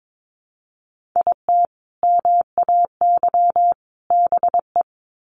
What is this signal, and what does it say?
Morse code, heard as a keyed tone, about 22 wpm, 700 Hz. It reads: IT MAY BE
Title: Sets of 3 Words 22wpm